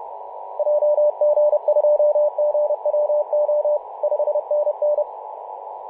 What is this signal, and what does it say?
Morse code, heard as a keyed tone, about 31 wpm, 600 Hz, with radio noise. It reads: JG2GWO 5NN
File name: Mar/20 2349z FW5JJ 28.011MHz CW